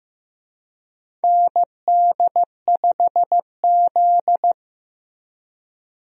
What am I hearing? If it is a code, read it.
Morse code, heard as a keyed tone, about 15 wpm, 700 Hz. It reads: ND5Z
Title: USA Call Signs - 4 Characters - Repeated 2x - 15wpm